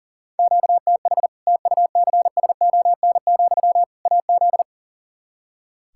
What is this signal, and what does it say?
Morse code, heard as a keyed tone, about 37 wpm, 700 Hz. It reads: QTH TUCSON, AZ
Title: Practice QSO - Exchange 1 - Beginning - 40wpm